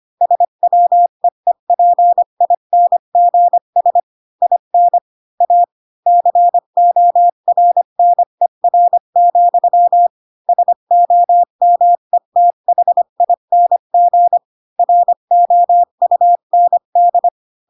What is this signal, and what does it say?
Morse code, heard as a keyed tone, about 25 wpm, 700 Hz. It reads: SWEEPINGS IN A CORNER, SOMETHING ROUND
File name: Pinocchio / Pinocchio ch5 20 25 ik1hge